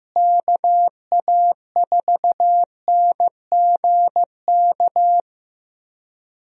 Morse code, heard as a keyed tone, about 15 wpm, 700 Hz. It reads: KA4NGK
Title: USA Call Signs - 6 Characters - Repeated 3x - 15wpm